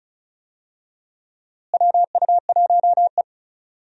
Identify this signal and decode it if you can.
Morse code, heard as a keyed tone, about 35 wpm, 700 Hz. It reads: WU1E
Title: USA Call Signs - 4 Characters - Repeated 3x - 35wpm